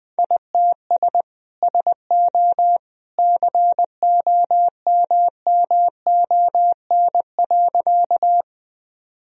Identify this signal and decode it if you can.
Morse code, heard as a keyed tone, about 20 wpm, 700 Hz. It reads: ITS SO COMMON.
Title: Sentences from Top 1000 Words - Review Part 3 - 20wpm